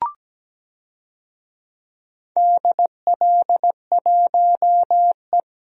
Morse code, heard as a keyed tone, about 17 wpm, 700 Hz. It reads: DL1E
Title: DXCC-100 Call Signs - 4 Characters - 17wpm